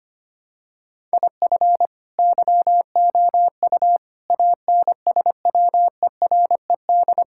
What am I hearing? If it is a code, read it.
Morse code, heard as a keyed tone, about 25 wpm, 700 Hz. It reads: IF YOU ANSWERED
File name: Sets of 9 Words - 25wpm